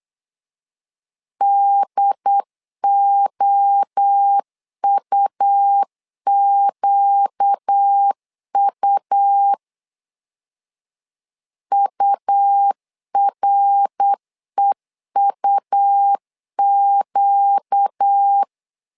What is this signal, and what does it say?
Morse code, heard as a keyed tone, about 8 wpm, 800 Hz. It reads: DOUQU UREUQ
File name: LEZIONI / 4 Q 8wpm 2732